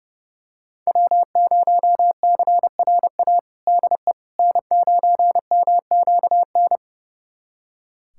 Morse code, heard as a keyed tone, about 30 wpm, 700 Hz. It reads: W0CRA DE N9MQN
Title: USA Callsign DE Callsign - 5 Characters - 30wpm